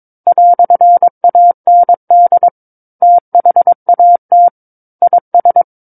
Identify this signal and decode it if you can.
Morse code, heard as a keyed tone, about 22 wpm, 700 Hz. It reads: "AND THAT IS
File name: Book One / 13-Chapter 13 - 22 WPM